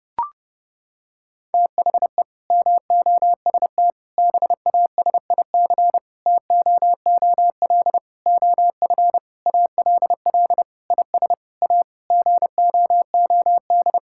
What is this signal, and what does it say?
Morse code, heard as a keyed tone, about 30 wpm, 700 Hz. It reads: THE MOST BASIC TOOL OF ALL IS A GOOD
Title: Sentences from Top 1000 Words - Review Part 6 - 30wpm